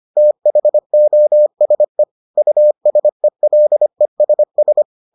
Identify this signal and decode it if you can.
Morse code, heard as a keyed tone, about 25 wpm, 600 Hz. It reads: THOSE USELESS